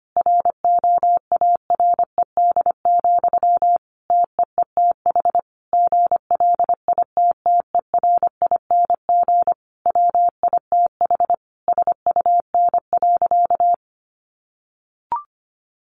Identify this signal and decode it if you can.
Morse code, heard as a keyed tone, about 25 wpm, 700 Hz. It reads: ROARED, TEETH GLITTERING WITH SUN.
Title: A Sound Of Thunder 25wpm